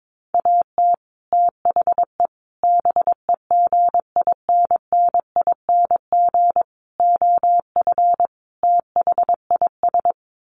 Morse code, heard as a keyed tone, about 22 wpm, 700 Hz. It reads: AT THE BEGINNING OF THIS
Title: Sets of 5 Words Encore 22wpm